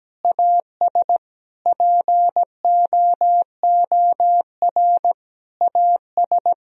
Morse code, heard as a keyed tone, about 17 wpm, 700 Hz. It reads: AS POOR AS
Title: Sentences from Top 1000 Words - Review Part 2 - 17wpm